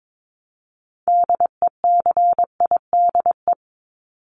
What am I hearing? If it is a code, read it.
Morse code, heard as a keyed tone, about 22 wpm, 700 Hz. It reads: DECIDE
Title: ED Suffix - Top 100 Words with Base Words 22wpm